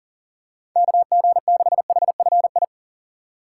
Morse code, heard as a keyed tone, about 41 wpm, 700 Hz. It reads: KG6HFI
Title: USA Call Signs - 3x Character Spacing - 40wpm